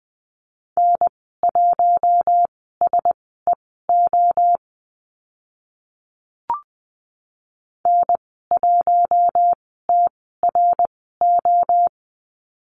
Morse code, heard as a keyed tone, about 20 wpm, spaced as for 14 wpm, 700 Hz. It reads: N1SEO N1TRO
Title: 082 Lesson 14g - Number 2 - Call Signs 20wpm